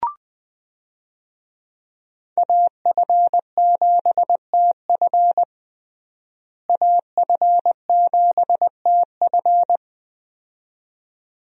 Morse code, heard as a keyed tone, about 21 wpm, 700 Hz. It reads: AF7TF AF7TF
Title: USA Call Signs - 5 Characters - Repeated 2x - 20wpm